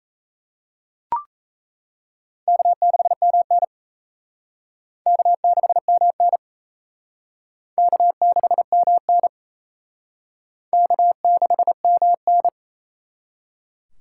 Morse code, heard as a keyed tone, about 35 wpm, 700 Hz. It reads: K6MN K6MN K6MN K6MN